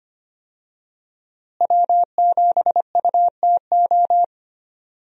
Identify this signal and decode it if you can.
Morse code, heard as a keyed tone, about 25 wpm, 700 Hz. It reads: W7UTO
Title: USA Call Signs - 5 Characters - Repeated 3x - 25wpm